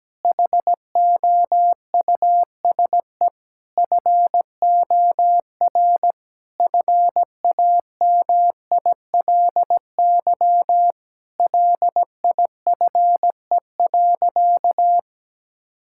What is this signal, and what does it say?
Morse code, heard as a keyed tone, about 17 wpm, 700 Hz. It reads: HOUSE FOR FAMILY LIFE.
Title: Sentences from Top 1000 Words - Review Part 5 - 17wpm